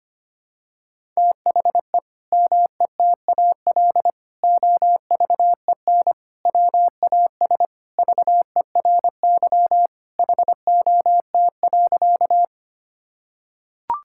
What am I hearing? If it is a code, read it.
Morse code, heard as a keyed tone, about 25 wpm, 700 Hz. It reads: THE METAL OVEN WAS VERY HOT.
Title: Sentences from Top 1000 Words - Review Part 6 - 25wpm